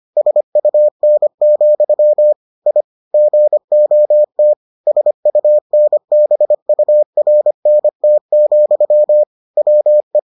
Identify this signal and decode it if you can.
Morse code, heard as a keyed tone, about 25 wpm, 600 Hz. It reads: SUN, I GOT SUNBURNT, WE